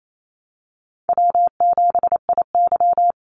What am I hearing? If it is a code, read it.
Morse code, heard as a keyed tone, about 28 wpm, 700 Hz. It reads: W7IY
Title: Easy Call Signs Repeated 2x - 28wpm